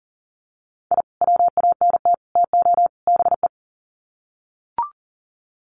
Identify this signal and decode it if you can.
Morse code, heard as a keyed tone, about 40 wpm, 700 Hz. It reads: I WANT TO BE
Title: Sets of 4 Words 40wpm